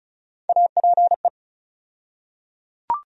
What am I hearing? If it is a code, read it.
Morse code, heard as a keyed tone, about 34 wpm, 700 Hz. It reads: APE
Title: USA Call Sign Postfixes - 3 Letters 35wpm